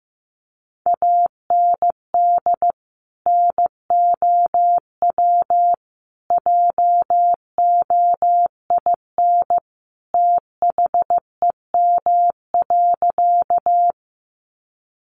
Morse code, heard as a keyed tone, about 15 wpm, 700 Hz. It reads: AND NOW JOIN THEM.
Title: Sentences from Top 1400 Words 15wpm